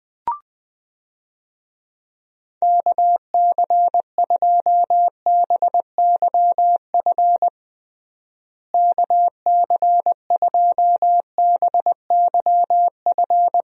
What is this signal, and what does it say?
Morse code, heard as a keyed tone, about 20 wpm, 700 Hz. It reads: KC2BYF KC2BYF
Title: USA Call Signs - 6 Characters - Repeated 3x - 20wpm